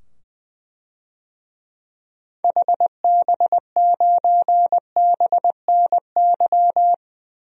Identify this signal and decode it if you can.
Morse code, heard as a keyed tone, about 20 wpm, 700 Hz. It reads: HB9BNY
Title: DXCC-100 Call Signs - 6 Characters - 20wpm